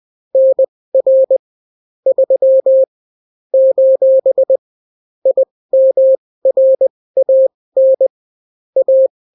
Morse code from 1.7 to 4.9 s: 3 :